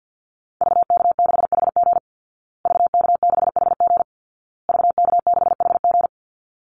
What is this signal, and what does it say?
Morse code, heard as a keyed tone, about 50 wpm, 700 Hz. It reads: VK6HG VK6HG VK6HG
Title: Call Signs Repeated 2x - 50wpm